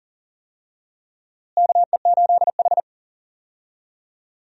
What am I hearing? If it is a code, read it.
Morse code, heard as a keyed tone, about 39 wpm, 700 Hz. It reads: KE8H
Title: USA Call Signs - 4 Characters - Repeated 3x - 40wpm